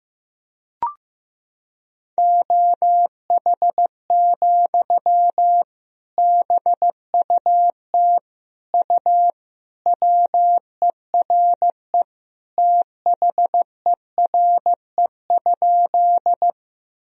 Morse code, heard as a keyed tone, about 15 wpm, 700 Hz. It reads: OH, BUT U WERE THERE?
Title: Sentences from Top 500 Words and Abbreviations 15wpm